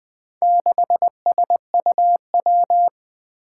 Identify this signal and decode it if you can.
Morse code, heard as a keyed tone, about 21 wpm, 700 Hz. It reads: 6SUW
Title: Four Challenging Letter-Number Combinations 20wpm